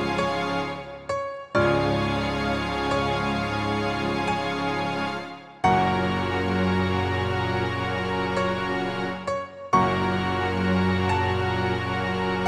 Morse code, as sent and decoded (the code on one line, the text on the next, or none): none